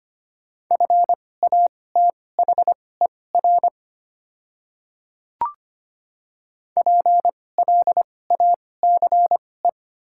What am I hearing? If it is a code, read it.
Morse code, heard as a keyed tone, about 25 wpm, spaced as for 18 wpm, 690 Hz. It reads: FATHER PLACE
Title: 2x Character Spacing - Top 200 Words - 25wpm